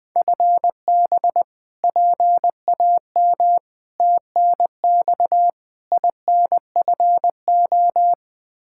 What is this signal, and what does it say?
Morse code, heard as a keyed tone, about 20 wpm, 700 Hz. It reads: FB PAM TNX INFO
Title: Practice QSO - Exchange 3 - More Info - 20wpm